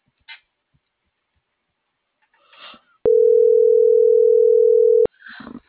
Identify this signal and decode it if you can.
Ringback tone (North America), 440 + 480 Hz.
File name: CENTRAL NMP / out melanie.diaznmp - .40193